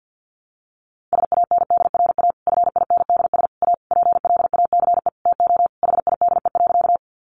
Morse code, heard as a keyed tone, about 48 wpm, 700 Hz. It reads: SANDRA FINDS A PLACE TO HIDE.
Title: Easy Phrases 50wpm